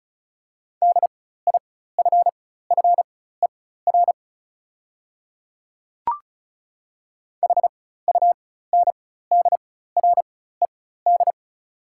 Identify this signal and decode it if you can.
Morse code, heard as a keyed tone, about 36 wpm, spaced as for 16 wpm, 690 Hz. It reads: DIFFER HUNDRED
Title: 4x Character Spacing - Top 500 Words - 35wpm